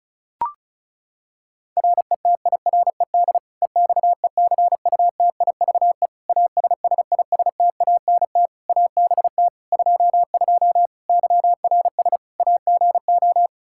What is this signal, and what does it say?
Morse code, heard as a keyed tone, about 34 wpm, 700 Hz. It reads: RETIRED EXECUTIVE ASSISTANT ABT 22 YRS AGO
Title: Practice QSO - Exchange 3 - More Info - 35wpm